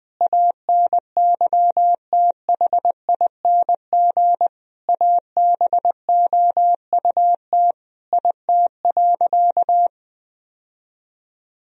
Morse code, heard as a keyed tone, about 20 wpm, 700 Hz. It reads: ANYTHING ABOUT IT.
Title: Sentences from Top 1000 Words - Review Part 2 - 20wpm